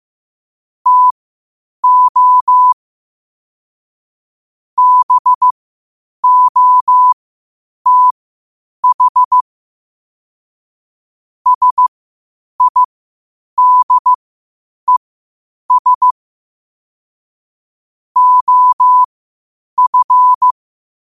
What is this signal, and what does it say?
Morse code, heard as a keyed tone, about 15 wpm, spaced as for 8 wpm, 1000 Hz. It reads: TO BOTH SIDES OF
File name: qotdfiles / 8wpm